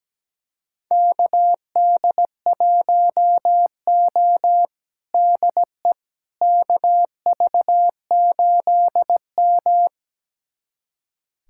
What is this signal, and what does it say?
Morse code, heard as a keyed tone, about 17 wpm, 700 Hz. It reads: KD1O DE KV8M
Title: USA Callsign DE Callsign - 4 Characters - 17wpm